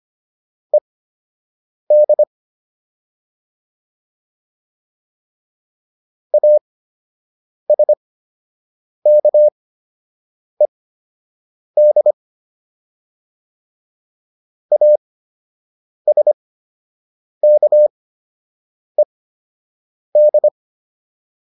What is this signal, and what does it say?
Morse code, heard as a keyed tone, about 25 wpm, spaced as for 7 wpm, 600 Hz. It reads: ED ASKED ASKED